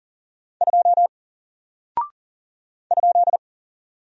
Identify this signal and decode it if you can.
Morse code, heard as a keyed tone, about 40 wpm, 700 Hz. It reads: 2 ?
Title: Single Character (Letter:Number:Punctuation:Prosign) Alt 40wpm